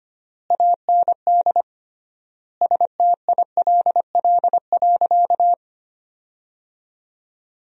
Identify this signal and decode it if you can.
Morse code, heard as a keyed tone, about 25 wpm, 700 Hz. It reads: AND STILL.
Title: 3x Word-Spacing - Sentences from Top 500 Words - 25wpm